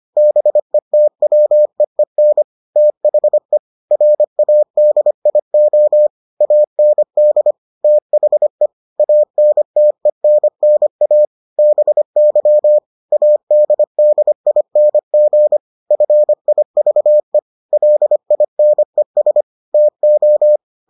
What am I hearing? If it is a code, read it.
Morse code, heard as a keyed tone, about 25 wpm, 600 Hz. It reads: BETWEEN THE RADIO AND THE ANTENNA BY ADDING FIVE LINES TO